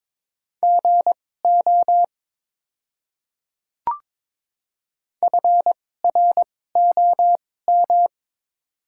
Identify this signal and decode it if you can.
Morse code, heard as a keyed tone, about 22 wpm, spaced as for 16 wpm, 700 Hz. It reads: GO FROM
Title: 2x Character Spacing - Top 100 Words - 22wpm